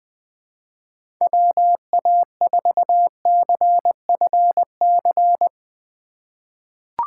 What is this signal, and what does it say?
Morse code, heard as a keyed tone, about 20 wpm, 700 Hz. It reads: WA4CFC
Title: USA Call Signs - 6 Characters - 4x Character Spacing - 20wpm